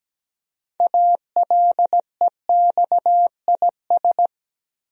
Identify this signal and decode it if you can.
Morse code, heard as a keyed tone, about 17 wpm, 700 Hz. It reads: ALEXIS
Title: USA Common Names - 17wpm